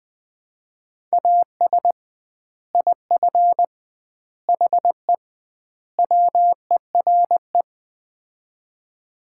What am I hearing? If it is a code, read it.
Morse code, heard as a keyed tone, about 21 wpm, 700 Hz. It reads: AS IF HE WERE
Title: 2x Word-Spacing - Sets of 4 Words - 20wpm